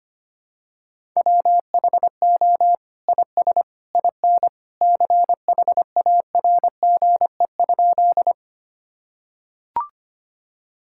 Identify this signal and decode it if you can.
Morse code, heard as a keyed tone, about 25 wpm, 700 Hz. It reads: WHO IS IN CHARGE?
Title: Sentences from Top 1000 Words - Review Part 6 - 25wpm